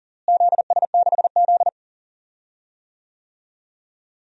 Morse code, heard as a keyed tone, about 40 wpm, 700 Hz. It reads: ZS6Z
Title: DXCC-100 Call Signs - 4 Characters - Repeated 2x - 40wpm